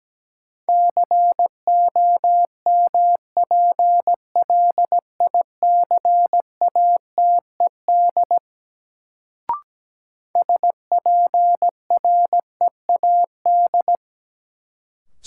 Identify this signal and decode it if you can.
Morse code, heard as a keyed tone, about 17 wpm, 700 Hz. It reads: COMPLICATED SPREAD
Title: Sentences from Top 1200 Words - New Word Drill Alt - 17wpm